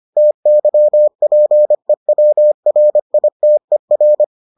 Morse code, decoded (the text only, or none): TYPEWRITER